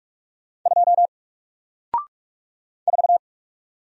Morse code, heard as a keyed tone, about 48 wpm, 700 Hz. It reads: 2 4
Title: Single Character (Letter:Number:Punctuation:Prosign) Alt 45wpm